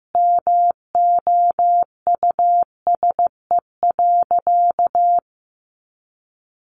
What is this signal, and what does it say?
Morse code, heard as a keyed tone, about 15 wpm, 700 Hz. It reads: MOUSE.
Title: A Sound Of Thunder 15wpm